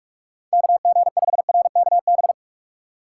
Morse code, heard as a keyed tone, about 46 wpm, 700 Hz. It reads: KG5RKB
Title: USA Call Signs - 6 Characters - 6x Character Spacing - 45wpm